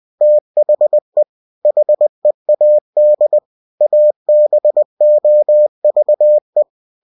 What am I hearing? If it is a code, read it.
Morse code, heard as a keyed tone, about 20 wpm, 600 Hz. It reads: THE HEAD ABOVE
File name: winnie-20wpm